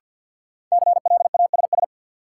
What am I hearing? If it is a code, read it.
Morse code, heard as a keyed tone, about 52 wpm, 700 Hz. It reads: CLASS
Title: Sentences from Top 1400 Words - New Word Drill Alt - 50wpm